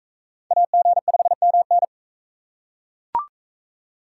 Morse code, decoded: AG5MN

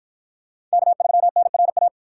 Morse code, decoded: K3NRA

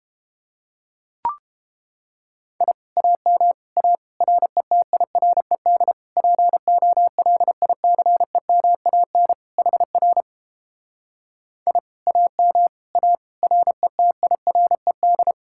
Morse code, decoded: I AM A RETIRED POLICEMAN HR I AM A RETIRED

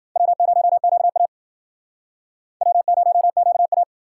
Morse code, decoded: W0QA W0QA

